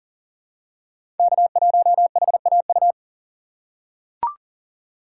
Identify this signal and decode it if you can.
Morse code, heard as a keyed tone, about 38 wpm, 700 Hz. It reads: K1HAU
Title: USA Call Signs - 5 Characters - 4x Character Spacing - 40wpm